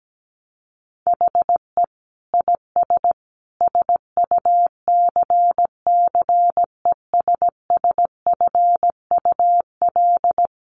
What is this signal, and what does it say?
Morse code, heard as a keyed tone, about 17 wpm, 700 Hz. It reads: HE IS SUCCESSFUL
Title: Sentences from Top 900 Words 17wpm